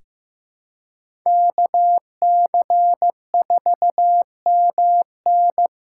Morse code, heard as a keyed tone, about 15 wpm, 700 Hz. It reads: KC4MN
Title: USA Call Signs - 5 Characters - 4x Character Spacing - 15wpm